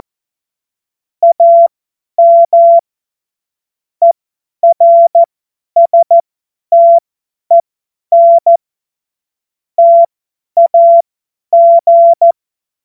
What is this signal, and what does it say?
Morse code, heard as a keyed tone, about 14 wpm, spaced as for 10 wpm, 680 Hz. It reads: AM ERSTEN TAG
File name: Morsekurs / Witze / Witz 24.Oktober 2023 50 F70